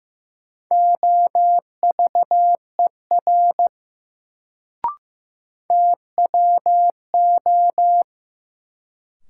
Morse code, heard as a keyed tone, about 15 wpm, 700 Hz. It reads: OVER TWO
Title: Sets of 2 Words - New Word Drill - 15wpm